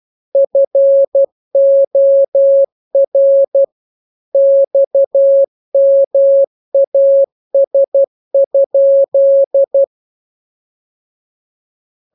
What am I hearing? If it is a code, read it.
Morse code, heard as a keyed tone, about 12 wpm, 550 Hz. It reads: FOR XMAS?
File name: jokes / joke 17 12 0000